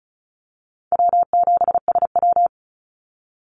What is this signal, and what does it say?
Morse code, heard as a keyed tone, about 34 wpm, 700 Hz. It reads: W7SW
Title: Call Signs Repeated 2x - 35wpm